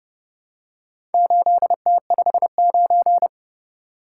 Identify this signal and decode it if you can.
Morse code, heard as a keyed tone, about 32 wpm, 700 Hz. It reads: 8T59